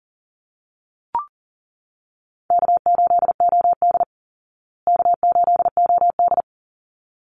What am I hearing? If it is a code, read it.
Morse code, heard as a keyed tone, about 40 wpm, 700 Hz. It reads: K8OD K8OD
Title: Call Signs Repeated 3x - 40wpm